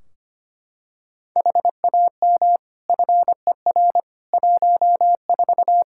Morse code, heard as a keyed tone, about 25 wpm, 700 Hz. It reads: HAM FER 14
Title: K1USN SST QSO Practice 25wpm